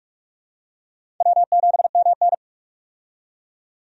Morse code, heard as a keyed tone, about 46 wpm, 700 Hz. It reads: W7MN
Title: USA Call Signs - 4 Characters - 45wpm